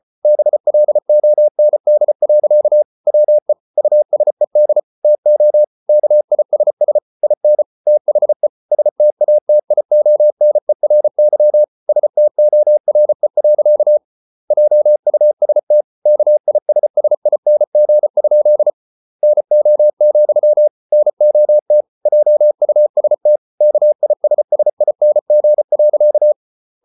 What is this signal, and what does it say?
Morse code, heard as a keyed tone, about 34 wpm, 600 Hz. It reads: BLOND. WE USED TO KISS IN THE STATIONERY STORE. JUST KISSING? NO, NOT JUST KISSING.